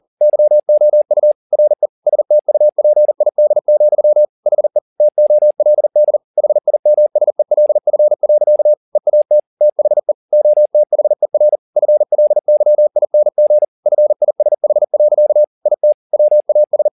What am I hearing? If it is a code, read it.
Morse code, heard as a keyed tone, about 40 wpm, 600 Hz. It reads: YOU RE STUPID, HE TOLD HIMSELF. EAT THE OTHER FLYING FISH. IT WAS
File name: the old man and the sea 2 40wpm